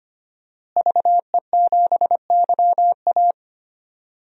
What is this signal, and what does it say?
Morse code, heard as a keyed tone, about 25 wpm, 700 Hz. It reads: VE7YA
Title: DXCC-100 Call Signs - 5 Characters - Repeated 2x - 25wpm